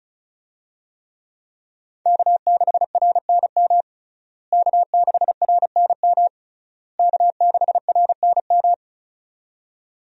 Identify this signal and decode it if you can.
Morse code, heard as a keyed tone, about 36 wpm, 700 Hz. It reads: K6RNM K6RNM K6RNM
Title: USA Call Signs - 5 Characters - Repeated 3x - 35wpm